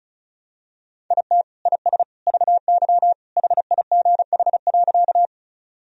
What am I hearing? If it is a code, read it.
Morse code, heard as a keyed tone, about 36 wpm, 700 Hz. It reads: IT IS VY HIGH.
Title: Sentences from Top 200 Words and Abbreviations 35wpm